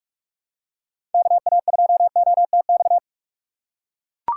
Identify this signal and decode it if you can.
Morse code, heard as a keyed tone, about 43 wpm, 700 Hz. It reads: KA2OTX
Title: USA Call Signs - 6 Characters - 3x Character Spacing - 45wpm